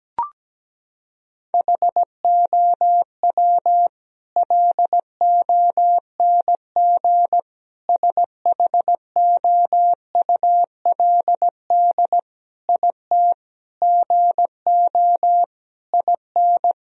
Morse code, heard as a keyed tone, about 17 wpm, 700 Hz. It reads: HOW LONG SHOULD IT GO IN
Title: Sentences from Top 1000 Words - Review Part 4 - 17wpm